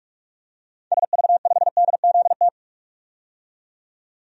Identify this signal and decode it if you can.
Morse code, heard as a keyed tone, about 46 wpm, 700 Hz. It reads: SV5DZT
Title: DXCC-100 Call Signs - 6 Characters - 45wpm